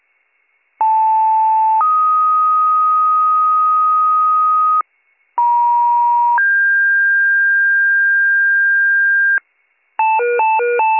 Two-tone page, heard as a one-second tone, then a three-second tone, about 880 then 1250 Hz, about 950 then 1600 Hz, with noise. It is heard from 0.8 to 9.4 s.